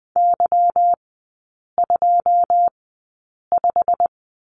Morse code, heard as a keyed tone, about 20 wpm, spaced as for 8 wpm, 690 Hz. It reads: Y25
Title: 145 Lesson 24e - Letter y - Three Characters 20wpm